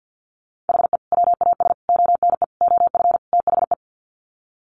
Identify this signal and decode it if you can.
Morse code, heard as a keyed tone, about 48 wpm, 700 Hz. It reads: HE WAS ONE OF THE
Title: Sets of 5 Words 50wpm